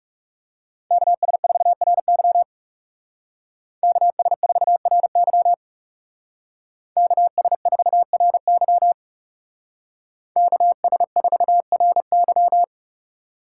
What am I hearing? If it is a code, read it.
Morse code, heard as a keyed tone, about 35 wpm, 700 Hz. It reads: KS4RY KS4RY KS4RY KS4RY
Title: Speed Racing - USA 2x2 Call Signs 30wpm